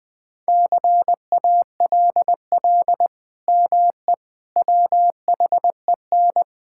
Morse code, heard as a keyed tone, about 20 wpm, 700 Hz. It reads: CALL ME WHEN
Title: Sentences with 2-Word Phrases - 20wpm